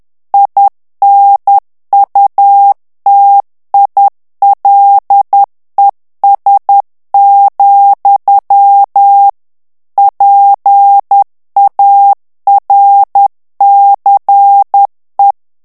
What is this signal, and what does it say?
Morse code, heard as a keyed tone, about 10 wpm, 800 Hz. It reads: INUTILES, PARCE
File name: morse mp3 / FRANCAIS TEXTE CLAIR 2 VITESSE 10 MOTS MINUTES